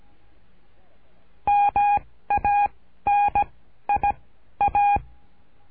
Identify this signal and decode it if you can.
Morse code, heard as a keyed tone, about 17 wpm, 820 Hz. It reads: M A N I A
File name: cursos / CURSO DE TELEGRAFIA / Telegrafia - Trilha 13